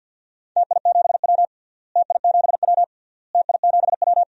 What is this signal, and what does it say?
Morse code, heard as a keyed tone, about 50 wpm, 700 Hz. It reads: TI7W TI7W TI7W
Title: DXCC-100 Call Signs - 4 Characters - Repeated 3x - 50wpm